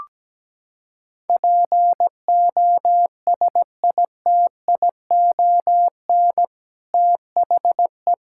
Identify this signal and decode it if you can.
Morse code, heard as a keyed tone, about 17 wpm, 700 Hz. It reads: POSITION THE
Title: Sentences from Top 1600 Words 17wpm